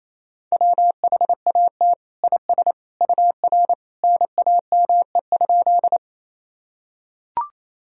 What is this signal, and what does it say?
Morse code, heard as a keyed tone, about 28 wpm, 700 Hz. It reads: WHAT IS UR NAME?
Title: Sentences from Top 200 Words and Abbreviations 28wpm